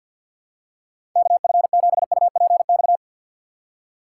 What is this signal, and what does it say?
Morse code, heard as a keyed tone, about 52 wpm, 700 Hz. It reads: KF7UPX
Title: USA Call Signs - 6 Characters - 4x Character Spacing - 50wpm